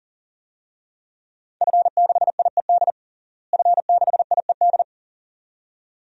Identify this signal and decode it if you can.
Morse code, heard as a keyed tone, about 40 wpm, 700 Hz. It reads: F6IED F6IED
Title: DXCC-100 Call Signs - 5 Characters - Repeated 2x - 40wpm